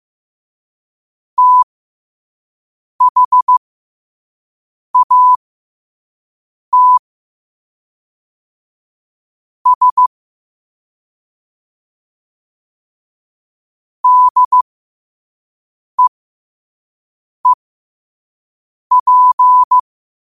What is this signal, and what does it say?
Morse code, heard as a keyed tone, about 15 wpm, spaced as for 5 wpm, 1000 Hz. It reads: THAT S DEEP
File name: qotdfiles / 5wpm